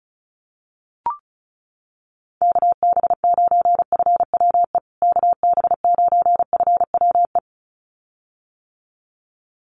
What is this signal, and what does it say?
Morse code, heard as a keyed tone, about 34 wpm, 700 Hz. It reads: KB9FWE KB9FWE
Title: USA Call Signs Repeated 2x 35wpm